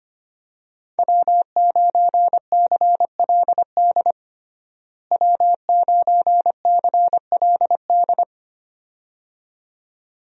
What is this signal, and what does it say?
Morse code, heard as a keyed tone, about 25 wpm, 700 Hz. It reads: W9CLD W9CLD
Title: USA Call Signs - 5 Characters - Repeated 2x - 25wpm